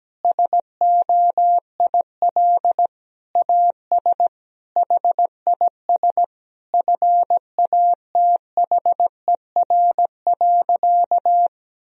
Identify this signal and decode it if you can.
Morse code, heard as a keyed tone, about 17 wpm, 700 Hz. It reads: SOIL AS HIS FATHER.
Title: Sentences from Top 1000 Words - Review Part 4 - 17wpm